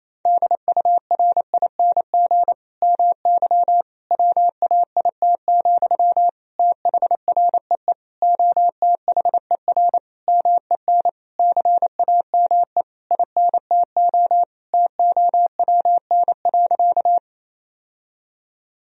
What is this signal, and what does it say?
Morse code, heard as a keyed tone, about 27 wpm, 700 Hz. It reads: DURING MY WAIT, THREE OTHER MEN CAME INTO TOWN.
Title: Sentences from Top 1000 Words - Review Part 2 - 28wpm